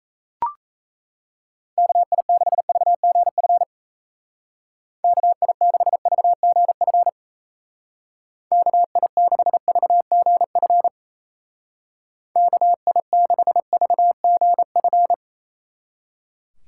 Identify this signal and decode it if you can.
Morse code, heard as a keyed tone, about 34 wpm, 700 Hz. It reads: KI6VGF KI6VGF KI6VGF KI6VGF
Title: Speed Racing - USA 2x3 Call Signs 28wpm